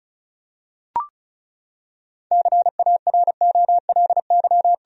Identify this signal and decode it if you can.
Morse code, heard as a keyed tone, about 34 wpm, 700 Hz. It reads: CAROLY
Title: USA Common Names - 35wpm